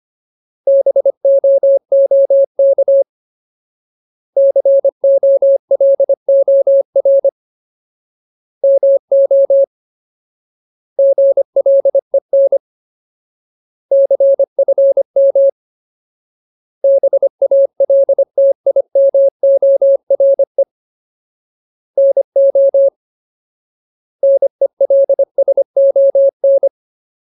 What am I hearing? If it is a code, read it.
Morse code, heard as a keyed tone, about 25 wpm, 550 Hz. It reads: BOOK COLOR MO GLEN CFM BALTIMORE NO NELSON
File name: Ultimate Word List / 25WPM Ultimate